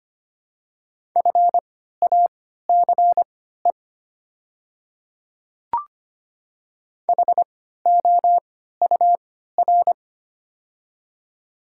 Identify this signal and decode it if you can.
Morse code, heard as a keyed tone, about 25 wpm, spaced as for 14 wpm, 690 Hz. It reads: FACE HOUR
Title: 3x Character Spacing - Top 500 Words - 25wpm